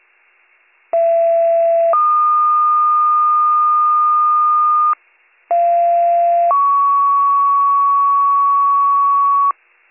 Two-tone page, heard as a one-second tone, then a three-second tone, about 670 then 1220 Hz, about 690 then 1120 Hz. It is heard from 0.9 to 9.5 s.